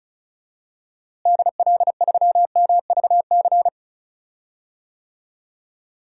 Morse code, heard as a keyed tone, about 36 wpm, 700 Hz. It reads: DL3MVC
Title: DXCC-100 Call Signs - 6 Characters - Repeated 3x - 35wpm